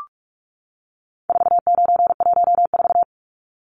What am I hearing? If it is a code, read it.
Morse code, heard as a keyed tone, about 44 wpm, 700 Hz. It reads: 4914
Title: Numbers - 4-Digit 45wpm